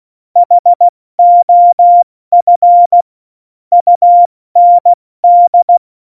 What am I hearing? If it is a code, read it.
Morse code, heard as a keyed tone, about 16 wpm, 700 Hz. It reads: HOF UND
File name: Morsekurs / Witze / Witz 28 April 2020, 70 F80